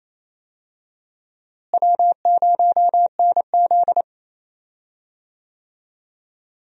Morse code, heard as a keyed tone, about 28 wpm, 700 Hz. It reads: W0NZ
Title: USA Call Signs - 4 Characters - Repeated 3x - 28wpm